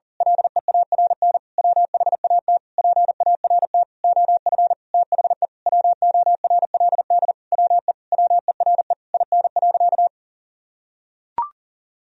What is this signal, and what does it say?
Morse code, heard as a keyed tone, about 40 wpm, 700 Hz. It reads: LEARN WHAT PART OF THE WORLD WE WERE IN.
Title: Sentences from Top 1000 Words - Review Part 3 - 40wpm